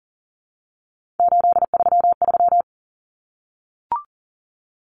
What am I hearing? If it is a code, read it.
Morse code, heard as a keyed tone, about 40 wpm, 700 Hz. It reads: 833
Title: Numbers - 3-Digit 40wpm